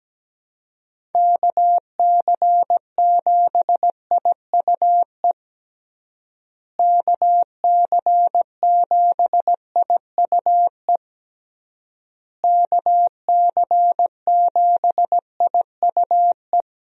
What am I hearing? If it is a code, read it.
Morse code, heard as a keyed tone, about 17 wpm, 700 Hz. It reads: KC7IUE KC7IUE KC7IUE
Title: USA Call Signs - 6 Characters - Repeated 3x - 17wpm